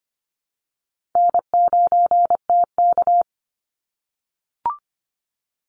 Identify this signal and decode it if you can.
Morse code, heard as a keyed tone, about 25 wpm, 700 Hz. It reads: N9TK
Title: Easy Call Signs - 25wpm